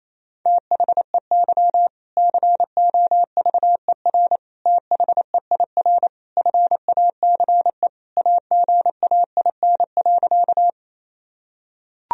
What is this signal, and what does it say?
Morse code, heard as a keyed tone, about 28 wpm, 700 Hz. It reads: THEY COVER THEIR FACE AGAIN.
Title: Sentences from Top 1000 Words - Review Part 1 - 28wpm